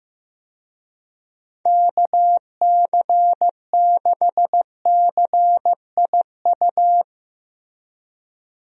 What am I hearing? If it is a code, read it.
Morse code, heard as a keyed tone, about 15 wpm, 700 Hz. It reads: KC6CIU
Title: USA Call Signs - 6 Characters - Repeated 3x - 15wpm